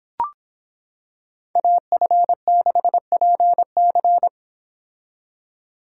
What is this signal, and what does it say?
Morse code, heard as a keyed tone, about 25 wpm, 700 Hz. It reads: AF6PC